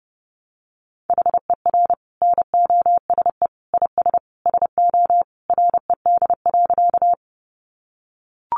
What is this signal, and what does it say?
Morse code, heard as a keyed tone, about 28 wpm, 700 Hz. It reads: HER NOSE IS SO RED.
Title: Easy Phrases 30wpm